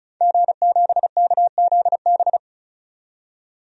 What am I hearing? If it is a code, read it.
Morse code, heard as a keyed tone, about 36 wpm, 700 Hz. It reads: G7KZB
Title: DXCC-100 Call Signs - 5 Characters - 35wpm